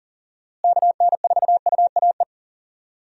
Morse code, heard as a keyed tone, about 40 wpm, 700 Hz. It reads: KN4UAE
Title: USA Call Signs - 6 Characters - Repeated 3x - 40wpm